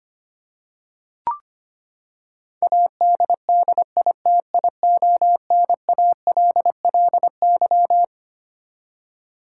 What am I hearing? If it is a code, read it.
Morse code, heard as a keyed tone, about 25 wpm, 700 Hz. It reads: ADDITIONALLY
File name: Twelve Letter Words - 25wpm